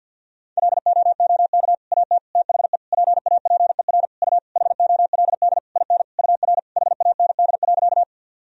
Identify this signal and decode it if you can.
Morse code, heard as a keyed tone, about 51 wpm, 700 Hz. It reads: LOOK AT THE PAPER U HOLD IN UR HAND.
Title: Sentences from Top 400 Words and Abbreviations 50wpm